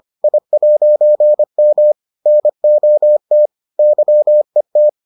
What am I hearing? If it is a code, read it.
Morse code, heard as a keyed tone, about 25 wpm, 600 Hz. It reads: I'M NOT YET